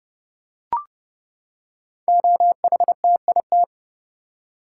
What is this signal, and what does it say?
Morse code, heard as a keyed tone, about 30 wpm, 700 Hz. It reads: OHTIT
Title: Code Groups - 5 Letters - 30wpm